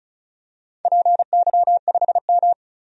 Morse code, heard as a keyed tone, about 34 wpm, 700 Hz. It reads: PY5M
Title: DXCC-100 Call Signs - 4 Characters - 35wpm